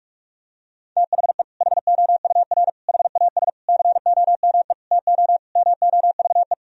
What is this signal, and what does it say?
Morse code, heard as a keyed tone, about 46 wpm, 700 Hz. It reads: THE HOUR HAS COME TO MOVE
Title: Sentences from Top 1000 Words - Review Part 1 - 45wpm